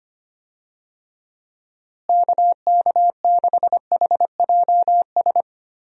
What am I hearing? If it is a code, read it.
Morse code, heard as a keyed tone, about 25 wpm, 700 Hz. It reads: KK6HJS
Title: USA Call Signs - 6 Characters - Repeated 2x - 25wpm